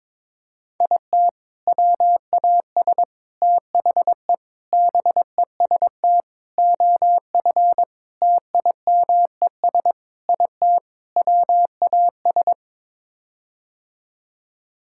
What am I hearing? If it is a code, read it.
Morse code, heard as a keyed tone, about 23 wpm, 700 Hz. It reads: IT WAS THE BEST OF TIMES IT WAS
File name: practice / 22wpm / Sets of 8 Words - 22wpm